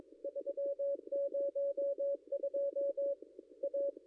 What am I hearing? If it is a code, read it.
Morse code, heard as a keyed tone, about 22 wpm, 560 Hz. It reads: 302 A
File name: contests / ss-cw-k5tr-op-n5rz -0800